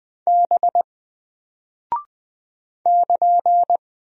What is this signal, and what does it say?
Morse code, heard as a keyed tone, about 20 wpm, 700 Hz. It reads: B <KN>
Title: Single Character (Letter:Number:Punctuation:Prosign) Alt 20wpm